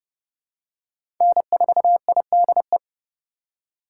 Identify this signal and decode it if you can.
Morse code, heard as a keyed tone, about 32 wpm, 700 Hz. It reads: N4IDE
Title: USA Call Signs - 5 Characters - 2x Character Spacing - 30wpm